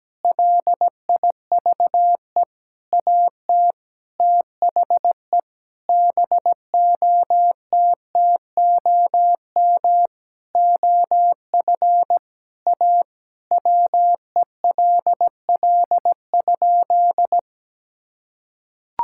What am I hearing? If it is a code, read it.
Morse code, heard as a keyed tone, about 17 wpm, 700 Hz. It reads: LIVE AT THE BOTTOM OF A WELL?
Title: Sentences from Top 1000 Words - Review Part 6 - 17wpm